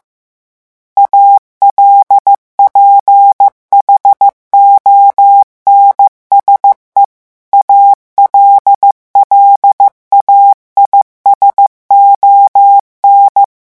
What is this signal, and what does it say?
Morse code, heard as a keyed tone, about 15 wpm, 800 Hz. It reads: ALPHONSE ALLAIS ON
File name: morse mp3 / revision9 citations humoristiques vitesse 14 mots minute